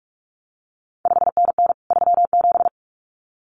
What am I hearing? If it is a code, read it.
Morse code, heard as a keyed tone, about 46 wpm, 700 Hz. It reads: 5NN 37
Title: CQ WW DX Contest Exchange Encore - 45wpm